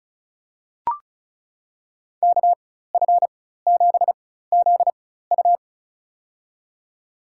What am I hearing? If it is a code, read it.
Morse code, heard as a keyed tone, about 34 wpm, spaced as for 16 wpm, 700 Hz. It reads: KF7ZU
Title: USA Call Signs - 4x Character Spacing - 35wpm